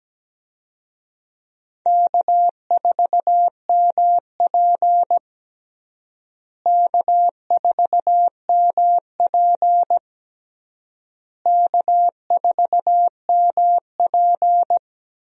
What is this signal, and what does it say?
Morse code, heard as a keyed tone, about 17 wpm, 700 Hz. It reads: K4MP K4MP K4MP
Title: USA Call Signs - 4 Characters - Repeated 3x - 17wpm